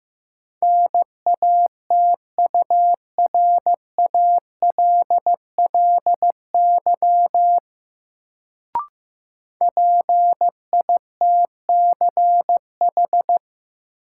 Morse code, heard as a keyed tone, about 15 wpm, 700 Hz. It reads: NATURALLY PITCH
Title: Sentences from Top 1300 Words - New Word Drill - 15wpm